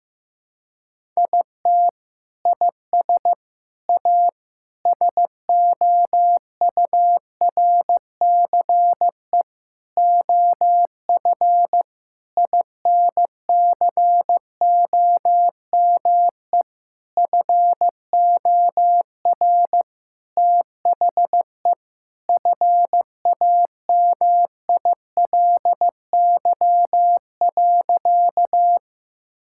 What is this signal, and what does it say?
Morse code, heard as a keyed tone, about 15 wpm, 700 Hz. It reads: IT IS A SOURCE OF INCOME FOR THE FAMILY.
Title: Sentences from Top 1000 Words - Review Part 5 - 15wpm